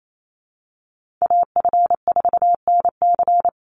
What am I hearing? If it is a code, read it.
Morse code, heard as a keyed tone, about 28 wpm, 700 Hz. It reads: AF4NC
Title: Call Signs - 28wpm